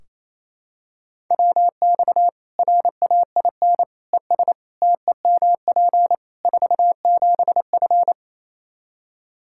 Morse code, heard as a keyed tone, about 28 wpm, 700 Hz. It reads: WX RAIN ES TEMP 47F
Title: Practice QSO - Exchange 2 - Info - 28wpm